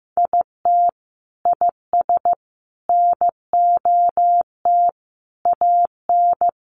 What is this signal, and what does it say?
Morse code, heard as a keyed tone, about 15 wpm, 700 Hz. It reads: IT IS NOT AN
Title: Sentences from Top 1500 Words 15wpm